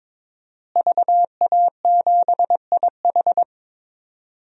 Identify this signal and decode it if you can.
Morse code, heard as a keyed tone, about 21 wpm, 700 Hz. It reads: VA7IH
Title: DXCC-100 Call Signs - 5 Characters - Repeated 2x - 22wpm